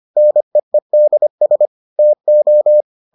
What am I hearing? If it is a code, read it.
Morse code, heard as a keyed tone, about 25 wpm, 600 Hz. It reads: NEEDS TO